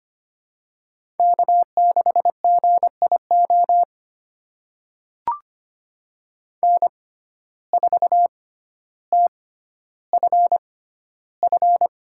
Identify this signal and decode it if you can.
Morse code, heard as a keyed tone, about 25 wpm, spaced as for 13 wpm, 700 Hz. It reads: K6GIO N4TFF
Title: USA Call Signs - 5 Characters - 6x Character Spacing - 25wpm